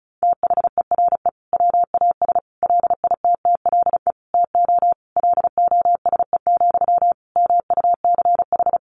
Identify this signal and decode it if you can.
Morse code, heard as a keyed tone, about 34 wpm, 700 Hz. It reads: THERE WAS LITTLE TO LOSE, MUCH
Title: Sentences from Top 1100 Words 35wpm